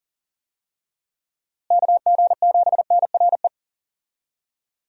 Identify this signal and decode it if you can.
Morse code, heard as a keyed tone, about 40 wpm, 700 Hz. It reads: KG7NRE
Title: USA Call Signs - 6 Characters - Repeated 3x - 40wpm